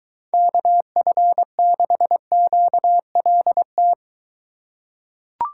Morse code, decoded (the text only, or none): KF6QLT